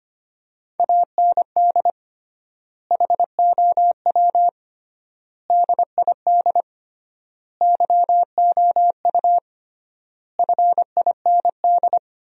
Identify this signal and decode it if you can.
Morse code, heard as a keyed tone, about 25 wpm, 700 Hz. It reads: AND HOW DID YOU FIND
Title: 3x Word-Spacing - Sentences from Top 100 Words - 25wpm